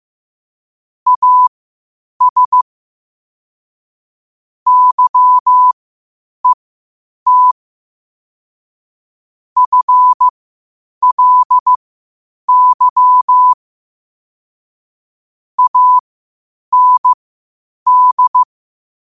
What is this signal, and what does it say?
Morse code, heard as a keyed tone, about 15 wpm, spaced as for 8 wpm, 1000 Hz. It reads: AS YET FLY AND
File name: qotdfiles / 8wpm